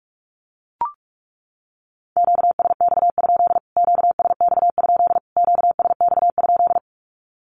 Morse code, heard as a keyed tone, about 44 wpm, 700 Hz. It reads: QSX? QSX? QSX?
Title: Q-Codes Repeated 3x 45wpm